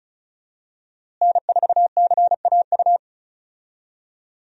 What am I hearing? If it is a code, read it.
Morse code, heard as a keyed tone, about 34 wpm, 700 Hz. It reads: N4CAU
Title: USA Call Signs - 5 Characters - Repeated 2x - 35wpm